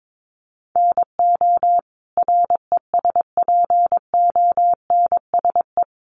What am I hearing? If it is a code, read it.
Morse code, heard as a keyed tone, about 22 wpm, 700 Hz. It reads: NO RESPONSE
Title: Sentences from Top 1400 Words 22wpm